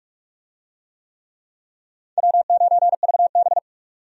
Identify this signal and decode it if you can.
Morse code, heard as a keyed tone, about 46 wpm, 700 Hz. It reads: W9VB
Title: USA Call Signs - 4 Characters - 45wpm